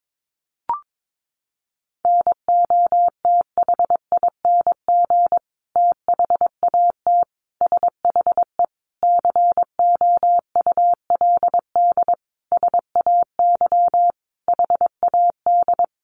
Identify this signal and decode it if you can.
Morse code, heard as a keyed tone, about 22 wpm, 700 Hz. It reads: NOTHING THAT SHE COULD SAY HAD
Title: Sentences from Top 1100 Words 22wpm